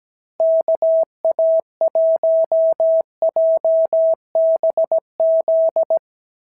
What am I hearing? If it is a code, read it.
Morse code, heard as a keyed tone, about 17 wpm, 650 Hz. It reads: KA1JBZ
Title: SOTA QSO Practice 17wpm